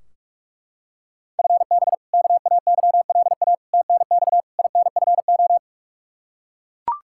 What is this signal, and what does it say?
Morse code, heard as a keyed tone, about 41 wpm, 700 Hz. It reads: FB KAYLA TNX INFO
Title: K1USN SST QSO Practice 45wpm